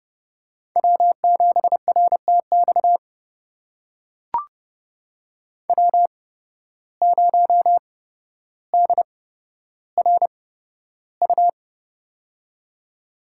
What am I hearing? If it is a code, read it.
Morse code, heard as a keyed tone, about 30 wpm, spaced as for 13 wpm, 700 Hz. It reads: W7RTX W0DRU
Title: USA Call Signs - 5 Characters - 8x Character Spacing - 30wpm